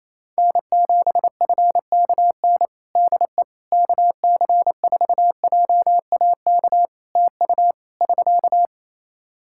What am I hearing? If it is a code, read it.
Morse code, heard as a keyed tone, about 27 wpm, 700 Hz. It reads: N7FKN DE KC4JAK TU <SK>
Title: Practice QSO - Exchange 4 - Ending - 28wpm